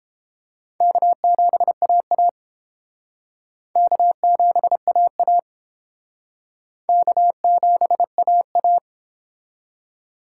Morse code, heard as a keyed tone, about 29 wpm, 700 Hz. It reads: K7AA K7AA K7AA